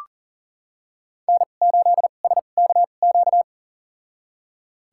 Morse code, heard as a keyed tone, about 40 wpm, spaced as for 29 wpm, 700 Hz. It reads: N8SKQ
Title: USA Call Signs - 2x Character Spacing - 40wpm